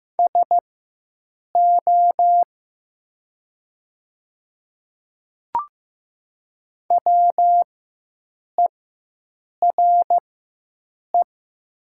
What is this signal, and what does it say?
Morse code, heard as a keyed tone, about 15 wpm, spaced as for 7 wpm, 700 Hz. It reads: SO WERE
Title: 4x Character Spacing - Top 100 Words - 15wpm